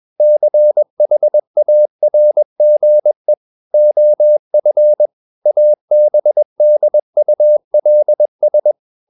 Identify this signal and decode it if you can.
Morse code, heard as a keyed tone, about 21 wpm, 600 Hz. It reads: CHARGE OF ABDULS
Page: CW med Gnister